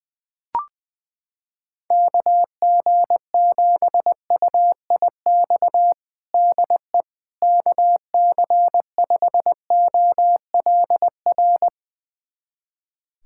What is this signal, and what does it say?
Morse code, heard as a keyed tone, about 21 wpm, 700 Hz. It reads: KG7UIX DE KC5OLR
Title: USA Callsign DE Callsign - 6 Characters - 20wpm